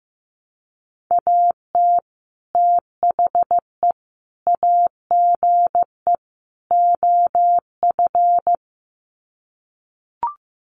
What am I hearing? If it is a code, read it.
Morse code, heard as a keyed tone, about 15 wpm, 700 Hz. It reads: AT THE AGE OF
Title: Sets of 4 Words 15wpm